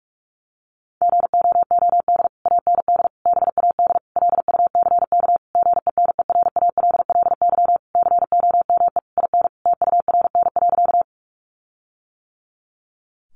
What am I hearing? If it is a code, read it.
Morse code, heard as a keyed tone, about 44 wpm, 700 Hz. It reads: GOOD AND BAD LUCK GENERALLY COME IN TURN.
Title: Sentences from Top 1500 Words 45wpm